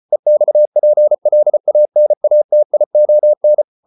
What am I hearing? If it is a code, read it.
Morse code, heard as a keyed tone, about 34 wpm, 600 Hz. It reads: EXPLANATION